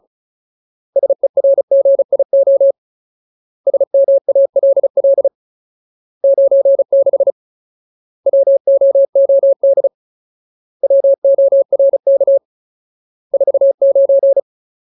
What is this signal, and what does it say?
Morse code, heard as a keyed tone, about 34 wpm, 550 Hz. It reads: SERGIO SMALL 96 WOOD WORK 49
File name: Ultimate Word List / 35WPM Ultimate